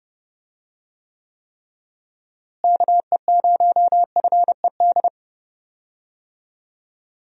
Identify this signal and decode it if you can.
Morse code, heard as a keyed tone, about 30 wpm, 700 Hz. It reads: KE0FED
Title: USA Call Signs - 6 Characters - Repeated 3x - 30wpm